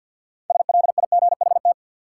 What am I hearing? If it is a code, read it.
Morse code, heard as a keyed tone, about 52 wpm, 700 Hz. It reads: SLIGHT
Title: Six Letter Words - 50wpm